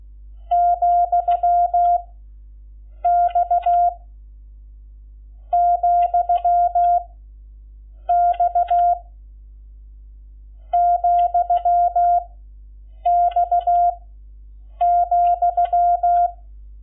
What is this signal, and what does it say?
Morse code, heard as a keyed tone, about 16 wpm, 700 Hz. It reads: , X , X , X ,
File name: Lecciones / CLASE Nº11